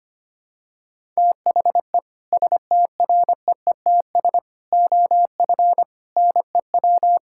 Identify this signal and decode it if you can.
Morse code, heard as a keyed tone, about 25 wpm, 700 Hz. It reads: THE STREETS OF NEW
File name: practice / 25wpm / Sentences with 3-Word Phrases - 25wpm